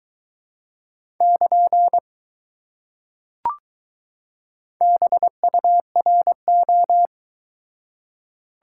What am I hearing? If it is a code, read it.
Morse code, heard as a keyed tone, about 21 wpm, 700 Hz. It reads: <KN> BURO